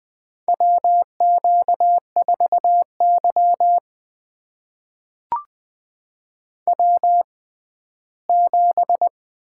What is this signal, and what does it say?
Morse code, heard as a keyed tone, about 20 wpm, spaced as for 14 wpm, 700 Hz. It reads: WQ4Y W7
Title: USA Call Signs - 4 Characters - 6x Character Spacing - 20wpm